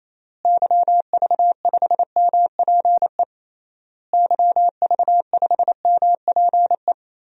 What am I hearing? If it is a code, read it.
Morse code, heard as a keyed tone, about 28 wpm, 700 Hz. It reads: YV5MPE YV5MPE
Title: DXCC-100 Call Signs - 6 Characters - Repeated 3x - 28wpm